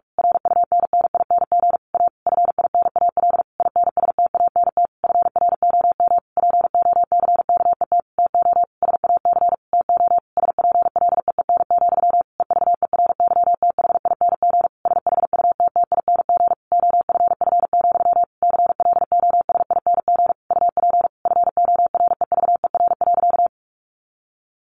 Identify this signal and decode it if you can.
Morse code, heard as a keyed tone, about 46 wpm, 700 Hz. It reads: RUNNING A FINAL INSTANT FROM POCKET TO SAC TO SPLEEN, EVERYTHING SHUTTING OFF, CLOSING UP FOREVER.
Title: A Sound Of Thunder 45wpm